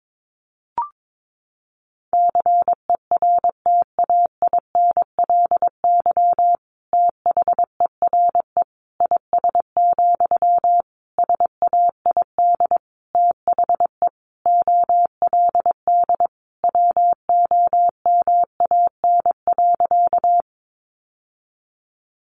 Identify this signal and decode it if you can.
Morse code, heard as a keyed tone, about 22 wpm, 700 Hz. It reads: CERTAINLY THERE IS, SAID THE OLD WOMAN.
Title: Sentences from Top 1100 Words 22wpm